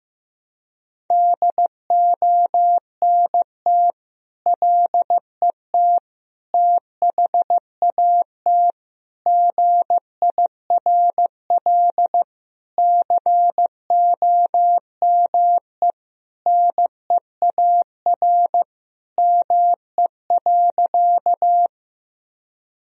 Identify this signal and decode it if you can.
Morse code, heard as a keyed tone, about 15 wpm, 700 Hz. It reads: DONT LET THAT GIRL COME NEAR ME.
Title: Sentences from Top 1000 Words - Review Part 1 - 15wpm